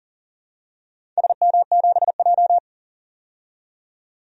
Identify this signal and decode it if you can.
Morse code, heard as a keyed tone, about 39 wpm, 700 Hz. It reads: SM7J
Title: DXCC-100 Call Signs - 4 Characters - 40wpm